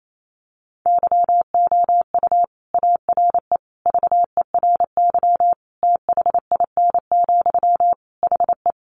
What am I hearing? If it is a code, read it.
Morse code, heard as a keyed tone, about 28 wpm, 700 Hz. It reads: YOU ARE VERY THIN, HE
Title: Sentences from Top 1200 Words 28wpm